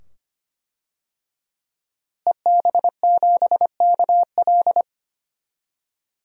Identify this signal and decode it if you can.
Morse code, heard as a keyed tone, about 25 wpm, 700 Hz. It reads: EB7KL
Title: DXCC-100 Call Signs - 5 Characters - Repeated 2x - 25wpm